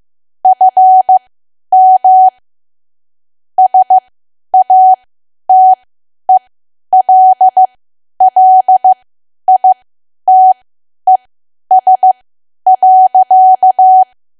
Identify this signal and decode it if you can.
Morse code, heard as a keyed tone, about 15 wpm, spaced as for 10 wpm, 750 Hz. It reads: FM SATELLITES.
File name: Archive / 10 WPM / 10WPM